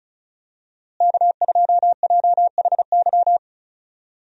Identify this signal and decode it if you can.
Morse code, heard as a keyed tone, about 36 wpm, 700 Hz. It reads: K2JHY
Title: USA Call Signs - 5 Characters - Repeated 2x - 35wpm